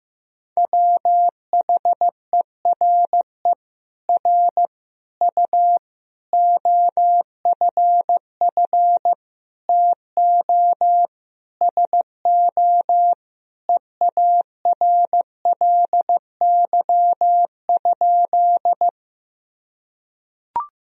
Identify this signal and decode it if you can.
Morse code, heard as a keyed tone, about 16 wpm, 700 Hz. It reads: WHERE R U OFF TO SO EARLY?
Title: Sentences from Top 400 Words and Abbreviations 15wpm